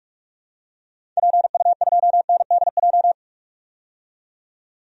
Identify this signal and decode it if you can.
Morse code, heard as a keyed tone, about 46 wpm, 700 Hz. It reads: PU2NDJ
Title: DXCC-100 Call Signs - 6 Characters - 45wpm